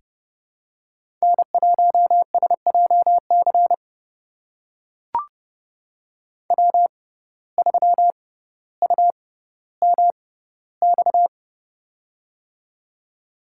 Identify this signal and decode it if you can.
Morse code, heard as a keyed tone, about 32 wpm, spaced as for 16 wpm, 700 Hz. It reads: N1SJC W3UMX
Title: USA Call Signs - 5 Characters - 6x Character Spacing - 30wpm